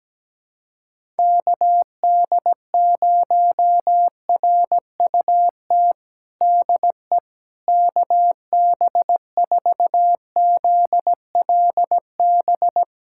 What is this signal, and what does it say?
Morse code, heard as a keyed tone, about 17 wpm, 700 Hz. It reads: KD0RUT DE KB4ZLB
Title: USA Callsign DE Callsign - 6 Characters - 17wpm